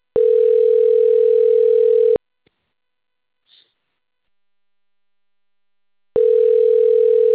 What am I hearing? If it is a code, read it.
Ringback tone (North America), 440 + 480 Hz.